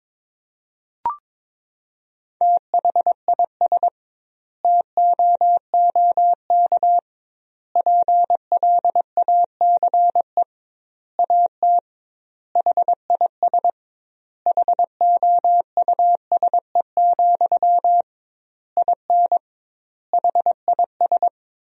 Morse code, heard as a keyed tone, about 22 wpm, 700 Hz. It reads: THIS TOOK PLACE AT HIS HOUSE, IN HIS
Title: 2x Word-Spacing - Sentences from Top 300 Words - 22wpm